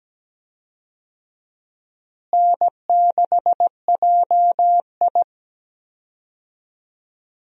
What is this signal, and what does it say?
Morse code, heard as a keyed tone, about 17 wpm, 700 Hz. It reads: N6JI
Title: USA Call Signs - 4 Characters - Repeated 3x - 17wpm